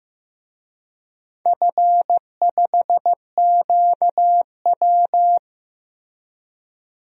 Morse code, heard as a keyed tone, about 15 wpm, 700 Hz. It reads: F5QW
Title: DXCC-100 Call Signs - 4 Characters - Repeated 3x - 15wpm